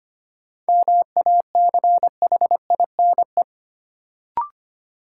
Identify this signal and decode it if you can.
Morse code, heard as a keyed tone, about 25 wpm, 700 Hz. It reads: MACHINE
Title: Sentences from Top 1000 Words - Review Part 2 - New Word Drill - 25wpm